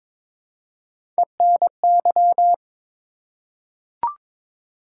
Morse code, heard as a keyed tone, about 22 wpm, 700 Hz. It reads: ENY
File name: ARRL Sections v2 - 22wpm